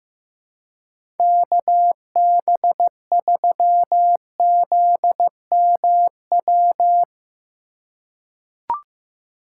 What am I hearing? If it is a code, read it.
Morse code, heard as a keyed tone, about 15 wpm, 700 Hz. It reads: KB3ZMW
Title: USA Call Signs - 6 Characters - 3x Character Spacing - 15wpm